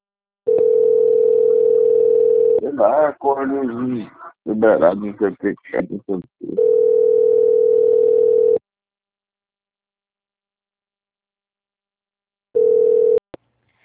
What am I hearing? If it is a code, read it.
Ringback tone (North America), 440 + 480 Hz, with speech.
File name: out- - - - - .36326